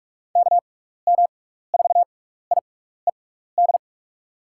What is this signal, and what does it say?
Morse code, heard as a keyed tone, about 42 wpm, spaced as for 15 wpm, 700 Hz. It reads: KM4IED
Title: USA Call Signs - 6 Characters - 6x Character Spacing - 45wpm